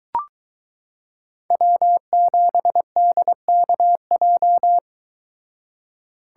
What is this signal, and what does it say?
Morse code, heard as a keyed tone, about 23 wpm, 700 Hz. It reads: W7DKJ